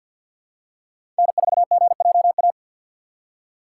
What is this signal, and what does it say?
Morse code, heard as a keyed tone, about 53 wpm, 700 Hz. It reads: N4GJA
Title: USA Call Signs - 5 Characters - 3x Character Spacing - 50wpm